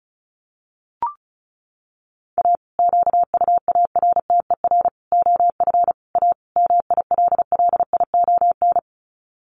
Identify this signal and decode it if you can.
Morse code, heard as a keyed tone, about 34 wpm, 700 Hz. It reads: A QUARTER OF A MILLION
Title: Sets of 5 Words Encore2 35wpm